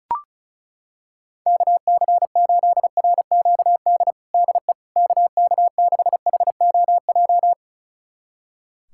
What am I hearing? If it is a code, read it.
Morse code, heard as a keyed tone, about 36 wpm, 700 Hz. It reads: KC8RQD DE KK6HOJ
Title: USA Callsign DE Callsign - 6 Characters - 35wpm